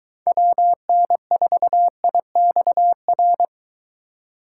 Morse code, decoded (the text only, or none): WN4IXR